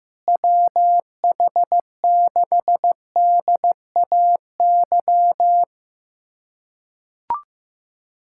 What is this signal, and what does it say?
Morse code, heard as a keyed tone, about 15 wpm, 700 Hz. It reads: WH6DAY
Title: USA Callsign DE Callsign - 6 Characters - 15wpm